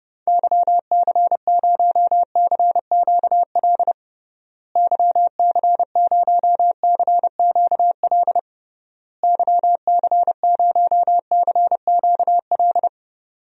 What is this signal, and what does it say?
Morse code, heard as a keyed tone, about 30 wpm, 700 Hz. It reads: YC0CQL YC0CQL YC0CQL
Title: DXCC-100 Call Signs - 6 Characters - Repeated 3x - 30wpm